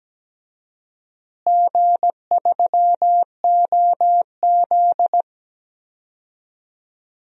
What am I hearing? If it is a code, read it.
Morse code, heard as a keyed tone, about 17 wpm, 700 Hz. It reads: G3OZ
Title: DXCC-100 Call Signs - 4 Characters - Repeated 2x - 17wpm